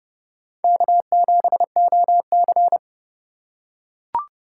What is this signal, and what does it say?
Morse code, heard as a keyed tone, about 30 wpm, 700 Hz. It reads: K7OC
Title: USA Call Signs - 4 Characters - 3x Character Spacing - 30wpm